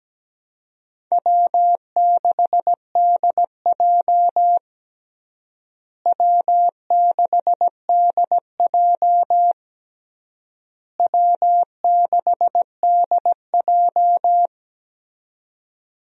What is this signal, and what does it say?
Morse code, heard as a keyed tone, about 17 wpm, 700 Hz. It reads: W6DJ W6DJ W6DJ
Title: USA Call Signs - 4 Characters - Repeated 3x - 17wpm